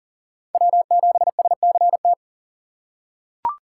Morse code, heard as a keyed tone, about 40 wpm, 700 Hz. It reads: W7SCT
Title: USA Call Signs - 3x Character Spacing - 40wpm